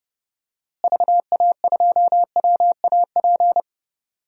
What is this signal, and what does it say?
Morse code, heard as a keyed tone, about 30 wpm, 700 Hz. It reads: VA2WAP
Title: DXCC-100 Call Signs - 6 Characters - Repeated 2x - 30wpm